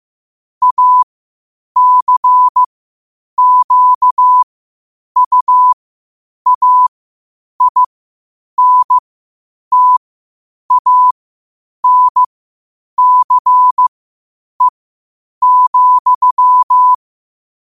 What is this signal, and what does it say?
Morse code, heard as a keyed tone, about 15 wpm, spaced as for 8 wpm, 1000 Hz. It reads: ACQUAINTANCE,
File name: qotdfiles / 8wpm